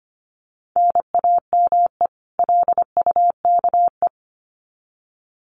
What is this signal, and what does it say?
Morse code, heard as a keyed tone, about 25 wpm, 700 Hz. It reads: NAME LUKE
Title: Common QSO Phrases 25wpm